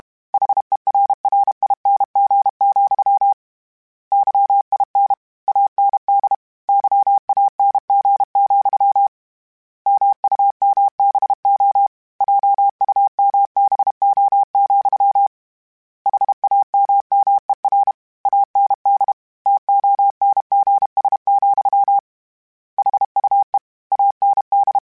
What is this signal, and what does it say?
Morse code, heard as a keyed tone, about 32 wpm, 800 Hz. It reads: HERRING, YIN AND YANG, MUMBO JUMBO, HAMMER AND TONGS, HUE AND
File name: Misc / Misc-mp3s / Word Pairs @30